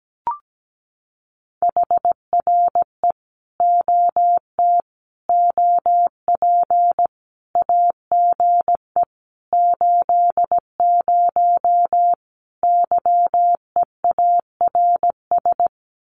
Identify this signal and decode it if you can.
Morse code, heard as a keyed tone, about 17 wpm, 700 Hz. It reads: HRE OT OP AGE 80 YEARS
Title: Common QSO Phrases 17wpm